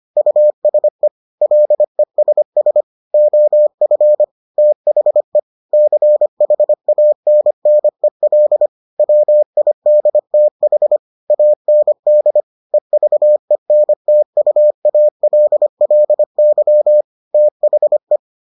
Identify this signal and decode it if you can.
Morse code, heard as a keyed tone, about 25 wpm, 600 Hz. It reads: USE LESS OF THE CHANNEL WIDTH AND EVENTUALLY THE